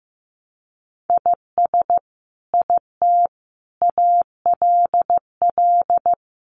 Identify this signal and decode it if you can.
Morse code, heard as a keyed tone, about 14 wpm, 700 Hz. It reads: IS IT ALL
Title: Sentences from Top 200 Words 15wpm